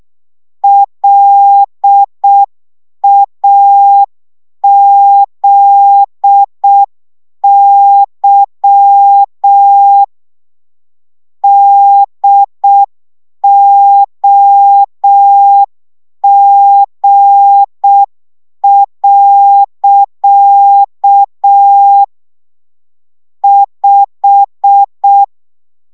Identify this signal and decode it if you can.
Morse code, heard as a keyed tone, about 6 wpm, 800 Hz. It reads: LAZY DOG. 5
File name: szamolo / CW 5 baud